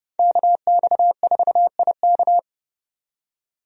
Morse code, heard as a keyed tone, about 30 wpm, 700 Hz. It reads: KX4IK
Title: USA Call Signs - 5 Characters - 30wpm